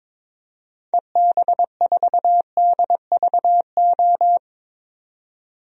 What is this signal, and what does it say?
Morse code, heard as a keyed tone, about 21 wpm, 700 Hz. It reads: EB4DVO
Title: DXCC-100 Call Signs - 6 Characters - 22wpm